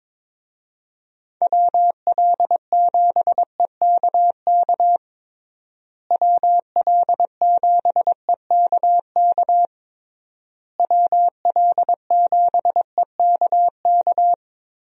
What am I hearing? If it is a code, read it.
Morse code, heard as a keyed tone, about 22 wpm, 700 Hz. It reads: WL7EKK WL7EKK WL7EKK
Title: USA Call Signs - 6 Characters - Repeated 3x - 22wpm